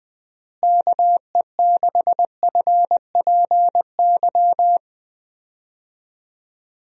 Morse code, heard as a keyed tone, about 20 wpm, 700 Hz. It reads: KE6FPY
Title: USA Call Signs - 6 Characters - Repeated 3x - 20wpm